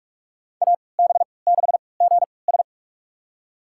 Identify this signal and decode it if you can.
Morse code, heard as a keyed tone, about 47 wpm, spaced as for 26 wpm, 700 Hz. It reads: AB6GS
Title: USA Call Signs - 5 Characters - 3x Character Spacing - 45wpm